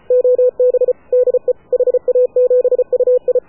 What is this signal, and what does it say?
Morse code, heard as a keyed tone, about 33 wpm, 500 Hz. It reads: OB DE HA7UI
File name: AUDIO2019 / HA7UI CW